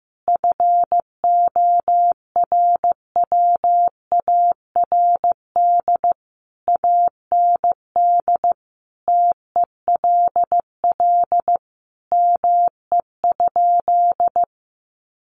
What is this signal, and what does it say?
Morse code, heard as a keyed tone, about 15 wpm, 700 Hz. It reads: FORWARD AND TELL ME?
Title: Sentences from Top 1300 Words 15wpm